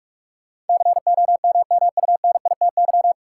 Morse code, heard as a keyed tone, about 46 wpm, 700 Hz. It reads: COMMUNITY
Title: Sentences from Top 1000 Words - Review Part 4 - New Word Drill - 45wpm